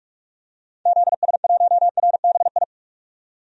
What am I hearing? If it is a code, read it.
Morse code, heard as a keyed tone, about 47 wpm, 700 Hz. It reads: ZS1RBI
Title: DXCC-100 Call Signs - 6 Characters - 45wpm